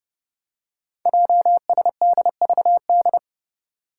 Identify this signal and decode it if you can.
Morse code, heard as a keyed tone, about 30 wpm, 700 Hz. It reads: JSDVD
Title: Code Groups - 5 Letters - 30wpm